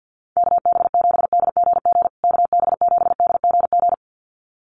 Morse code, heard as a keyed tone, about 50 wpm, 700 Hz. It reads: KB7DGG KB7DGG
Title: USA Call Signs Repeated 2x 50wpm